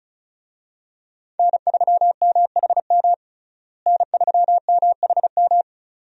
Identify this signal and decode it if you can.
Morse code, heard as a keyed tone, about 36 wpm, 700 Hz. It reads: N3MHM N3MHM
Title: USA Call Signs - 5 Characters - Repeated 3x - 35wpm